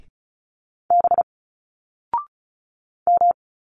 Morse code, heard as a keyed tone, about 36 wpm, 700 Hz. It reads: B M
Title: Single Letters 35wpm